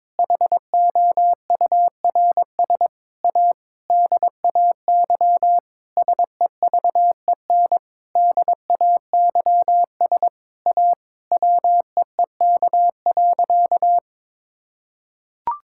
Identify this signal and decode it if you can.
Morse code, heard as a keyed tone, about 22 wpm, 700 Hz. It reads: HOURS A DAY SEVEN DAYS A WEEK.
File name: Sentences with 7-Word Phrases - 22wpm